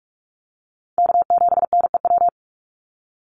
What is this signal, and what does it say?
Morse code, heard as a keyed tone, about 44 wpm, 690 Hz. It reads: K7NEW
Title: NA SOTA Chaser Callsigns 45wpm